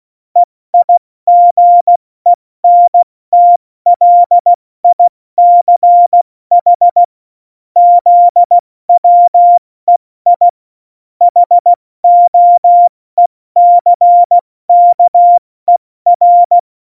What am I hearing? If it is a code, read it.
Morse code, heard as a keyed tone, about 16 wpm, spaced as for 14 wpm, 700 Hz. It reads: EIGENTLICH ZWEI HOECKER
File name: Morsekurs / Witze / Witz 06. November 2018, 70 F80